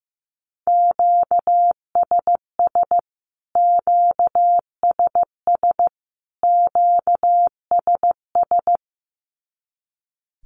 Morse code, heard as a keyed tone, about 15 wpm, 700 Hz. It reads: QSS QSS QSS
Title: Q-Codes Repeated 3x 15wpm